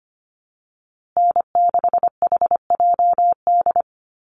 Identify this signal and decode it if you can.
Morse code, heard as a keyed tone, about 25 wpm, 700 Hz. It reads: N6HJD
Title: USA Call Signs Repeated 3x 25wpm